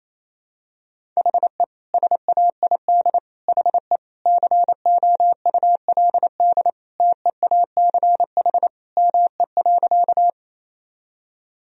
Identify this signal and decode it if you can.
Morse code, heard as a keyed tone, about 28 wpm, 700 Hz. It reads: HE SAID HE COULD TEACH ME.
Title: Sentences from Top 1000 Words - Review Part 2 - 28wpm